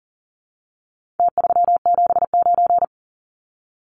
Morse code, heard as a keyed tone, about 40 wpm, 700 Hz. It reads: T379
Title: Numbers - Contest Style - 4-Digit 40wpm